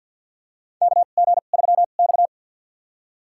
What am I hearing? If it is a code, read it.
Morse code, heard as a keyed tone, about 53 wpm, spaced as for 36 wpm, 700 Hz. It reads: KG3X
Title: USA Call Signs - 4 Characters - 2x Character Spacing - 50wpm